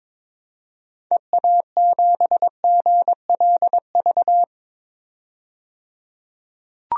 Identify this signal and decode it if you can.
Morse code, heard as a keyed tone, about 22 wpm, 700 Hz. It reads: EA7GLV
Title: DXCC-100 Call Signs - 6 Characters - 22wpm